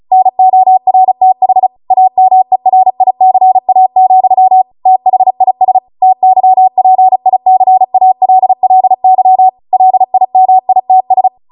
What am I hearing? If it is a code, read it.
Morse code, heard as a keyed tone, about 36 wpm, 750 Hz. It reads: NORTH AMERICA, THIS TYPICALLY LIMITS